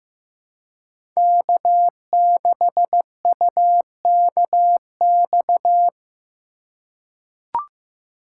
Morse code, heard as a keyed tone, about 15 wpm, 700 Hz. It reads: K6UKX
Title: USA Call Signs - 4x Character Spacing - 15wpm